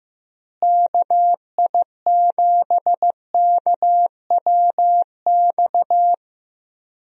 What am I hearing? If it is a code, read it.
Morse code, heard as a keyed tone, about 15 wpm, 700 Hz. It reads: KI7KWX
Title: USA Call Signs - 6 Characters - Repeated 3x - 15wpm